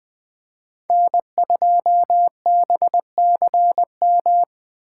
Morse code, heard as a keyed tone, about 20 wpm, 700 Hz. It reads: N2BCM
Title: Speed Racing - USA 1x3 Call Signs 17wpm